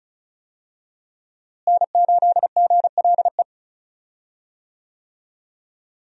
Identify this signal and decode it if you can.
Morse code, heard as a keyed tone, about 36 wpm, 700 Hz. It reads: N8GLE
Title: USA Call Signs - 5 Characters - Repeated 2x - 35wpm